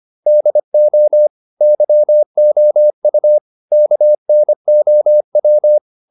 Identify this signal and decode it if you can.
Morse code, heard as a keyed tone, about 25 wpm, 600 Hz. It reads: DO YOU KNOW